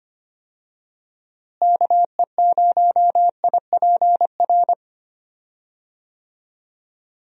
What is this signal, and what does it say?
Morse code, heard as a keyed tone, about 25 wpm, 700 Hz. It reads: KE0IPR
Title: USA Call Signs - 6 Characters - Repeated 3x - 25wpm